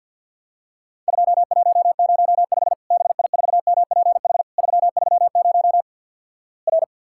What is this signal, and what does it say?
Morse code, heard as a keyed tone, about 51 wpm, 700 Hz. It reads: 2105 BI4MPH 330 R
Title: Worked All Europe DX Contest - QTC Exchange with Repeats - 50wpm